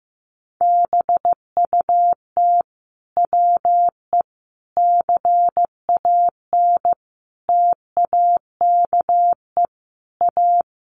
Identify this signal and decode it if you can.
Morse code, heard as a keyed tone, about 15 wpm, 700 Hz. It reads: BUT WE CAN TAKE A
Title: A Sound Of Thunder 15wpm